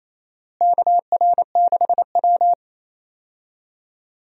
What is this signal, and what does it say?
Morse code, heard as a keyed tone, about 28 wpm, 700 Hz. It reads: KR6W
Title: USA Call Signs - 4 Characters - Repeated 2x - 28wpm